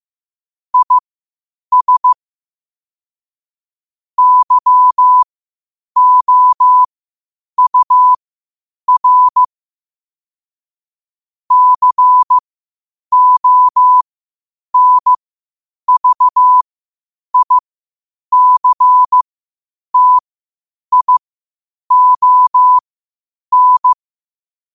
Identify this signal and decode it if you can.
Morse code, heard as a keyed tone, about 15 wpm, spaced as for 8 wpm, 1000 Hz. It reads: IS YOUR CONVICTION